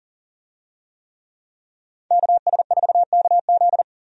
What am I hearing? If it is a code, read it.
Morse code, heard as a keyed tone, about 40 wpm, 700 Hz. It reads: KS4KZ
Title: USA Call Signs - 5 Characters - 40wpm